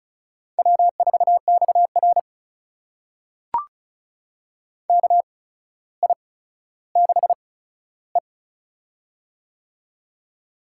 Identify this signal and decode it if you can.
Morse code, heard as a keyed tone, about 36 wpm, spaced as for 15 wpm, 700 Hz. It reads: W4XR KI6E
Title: USA Call Signs - 8x Character Spacing - 35wpm